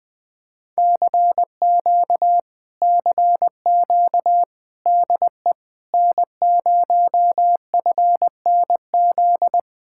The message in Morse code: CQ CQ DE N0FNZ

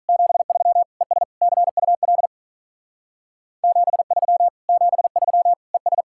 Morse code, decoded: 73 ES CUL 73 73 ES

